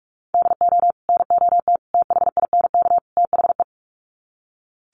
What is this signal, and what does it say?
Morse code, heard as a keyed tone, about 45 wpm, 700 Hz. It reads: DO NOT THINK THE
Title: Sets of 4 Words 45wpm